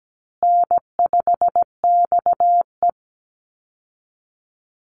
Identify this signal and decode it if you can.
Morse code, heard as a keyed tone, about 17 wpm, 700 Hz. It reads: N5XE
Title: Easy Call Signs Repeated 2x - 17wpm